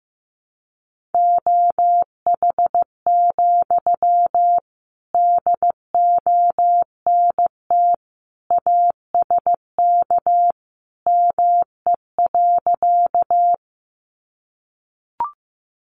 Morse code, heard as a keyed tone, about 15 wpm, 700 Hz. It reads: OH, DONT ASK ME.
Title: Sentences from Top 500 Words 15wpm